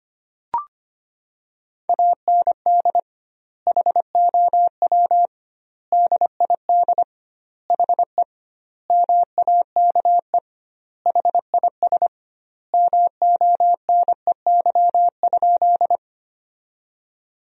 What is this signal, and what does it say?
Morse code, heard as a keyed tone, about 25 wpm, 700 Hz. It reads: AND HOW DID HE MAKE HIS MONEY?
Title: 2x Word-Spacing - Sentences from Top 400 Words - 25wpm